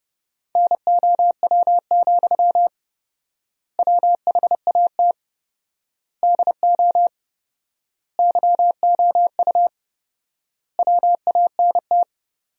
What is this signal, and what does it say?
Morse code, heard as a keyed tone, about 32 wpm, 700 Hz. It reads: NOW, WHAT DO YOU WANT
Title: 4x Word-Spacing - Sentences from Top 200 Words - 30wpm